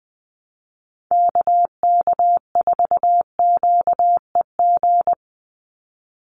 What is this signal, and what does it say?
Morse code, heard as a keyed tone, about 20 wpm, 700 Hz. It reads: KK4QEG
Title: USA Call Signs 20wpm